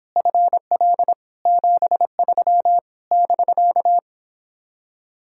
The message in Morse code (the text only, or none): FL 73 <BK>